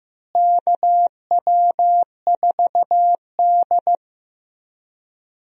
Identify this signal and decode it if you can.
Morse code, heard as a keyed tone, about 15 wpm, 700 Hz. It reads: KW4D
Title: USA Call Signs - 4 Characters - Repeated 3x - 15wpm